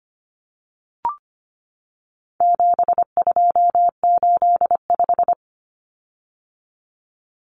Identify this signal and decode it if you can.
Morse code, heard as a keyed tone, about 25 wpm, 700 Hz. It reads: 7285
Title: Common Cut Numbers - 4-Digit 25wpm